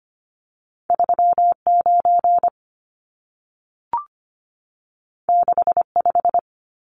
Morse code, heard as a keyed tone, about 25 wpm, 700 Hz. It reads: 39 65
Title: Numbers - 2-Digit 25wpm